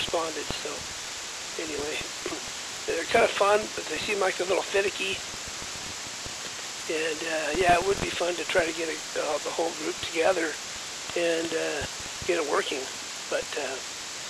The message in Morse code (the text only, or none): none